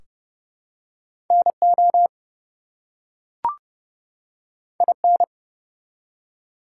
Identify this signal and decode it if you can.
Morse code, heard as a keyed tone, about 30 wpm, 700 Hz. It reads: NO IN
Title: Two Letter Words - 30wpm